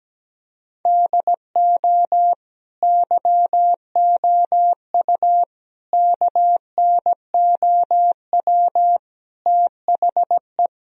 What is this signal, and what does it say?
Morse code, heard as a keyed tone, about 17 wpm, 700 Hz. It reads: DO YOU KNOW THE
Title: Sentences from Top 1000 Words - Review Part 4 - 17wpm